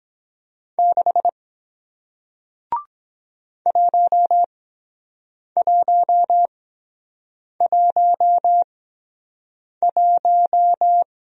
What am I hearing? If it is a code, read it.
Morse code, heard as a keyed tone, about 23 wpm, spaced as for 7 wpm, 700 Hz. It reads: 6 1111